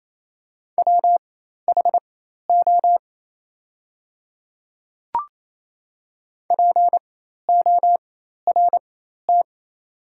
Morse code, heard as a keyed tone, about 28 wpm, spaced as for 13 wpm, 690 Hz. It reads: WHO PORT
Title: 4x Character Spacing - Top 200 Words - 28wpm